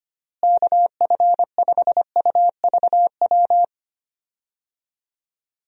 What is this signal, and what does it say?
Morse code, heard as a keyed tone, about 25 wpm, 700 Hz. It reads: KF5UVW
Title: USA Call Signs - 6 Characters - Repeated 2x - 25wpm